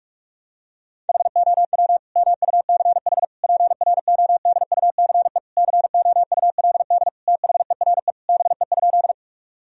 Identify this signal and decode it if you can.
Morse code, heard as a keyed tone, about 46 wpm, 700 Hz. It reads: HOW MUCH PRODUCE COULD THERE BE?
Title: Sentences from Top 1000 Words - Review Part 3 - 45wpm